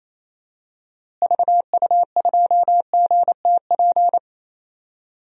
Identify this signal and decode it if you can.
Morse code, heard as a keyed tone, about 28 wpm, 700 Hz. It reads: VU2GTP
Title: DXCC-100 Call Signs - 6 Characters - Repeated 3x - 28wpm